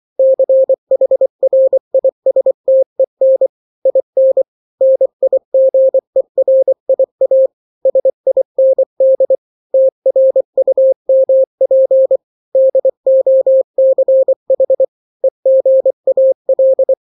Morse code, heard as a keyed tone, about 24 wpm, 530 Hz. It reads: CHRISTEN IN NIGERIA SIND TRUMP DOCH EGAL
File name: daily / 23wpm